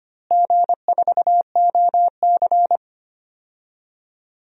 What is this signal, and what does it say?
Morse code, heard as a keyed tone, about 25 wpm, 700 Hz. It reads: G4OC
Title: DXCC-100 Call Signs - 4 Characters - Repeated 3x - 25wpm